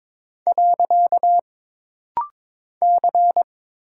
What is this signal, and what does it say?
Morse code, heard as a keyed tone, about 22 wpm, 700 Hz. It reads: . C
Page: This rapid-fire timing is geared to push your ICR (Instant Character Recognition) skills.